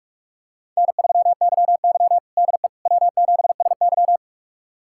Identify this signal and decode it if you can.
Morse code, heard as a keyed tone, about 46 wpm, 700 Hz. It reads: N3YY DE W7SY
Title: USA Callsign DE Callsign - 4 Characters - 45wpm